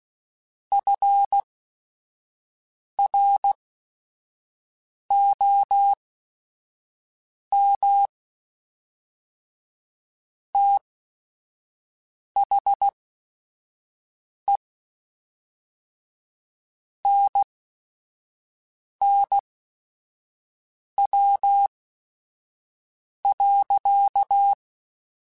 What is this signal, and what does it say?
Morse code, heard as a keyed tone, about 16 wpm, spaced as for 5 wpm, 800 Hz. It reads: FROM THE NNW.